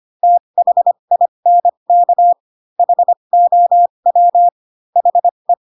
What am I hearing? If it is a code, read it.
Morse code, heard as a keyed tone, about 25 wpm, spaced as for 22 wpm, 700 Hz. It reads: THINK HOW HE